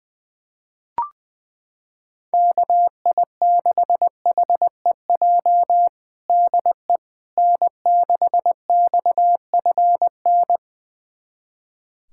Morse code, heard as a keyed tone, about 20 wpm, 700 Hz. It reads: KI6HEJ DE N6XFN
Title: USA Callsign DE Callsign 20wpm